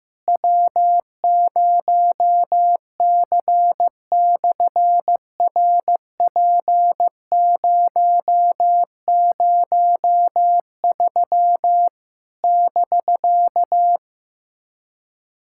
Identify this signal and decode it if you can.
Morse code, heard as a keyed tone, about 15 wpm, 700 Hz. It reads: W0C/RP003 <BK>
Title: SOTA QSO Practice 15wpm